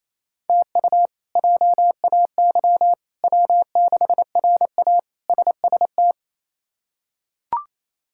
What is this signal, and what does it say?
Morse code, heard as a keyed tone, about 28 wpm, 700 Hz. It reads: TU JAY W6RA SST
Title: K1USN SST QSO Practice - 28wpm